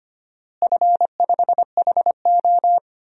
Morse code, heard as a keyed tone, about 25 wpm, 700 Hz. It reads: F5HO
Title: DXCC-100 Call Signs - 4 Characters - Repeated 3x - 25wpm